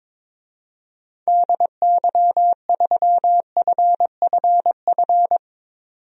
Morse code, heard as a keyed tone, about 22 wpm, 700 Hz. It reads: DY3FFF
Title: DXCC-100 Call Signs - 6 Characters - 22wpm